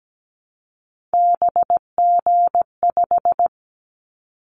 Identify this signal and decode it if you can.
Morse code, heard as a keyed tone, about 17 wpm, 700 Hz. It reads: BG5
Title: Callsign Prefixes for Top 100 DXCC - 17wpm